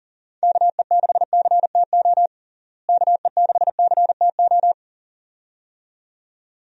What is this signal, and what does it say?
Morse code, heard as a keyed tone, about 37 wpm, 700 Hz. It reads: KE6CTO KE6CTO
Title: USA Call Signs - 6 Characters - Repeated 2x - 40wpm